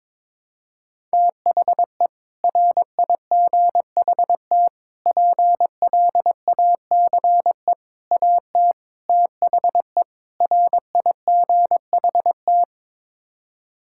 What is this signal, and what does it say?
Morse code, heard as a keyed tone, about 22 wpm, 700 Hz. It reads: THE RIGHT PLACE AT THE RIGHT
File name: practice / 22wpm / Sets of 6 Words - 22wpm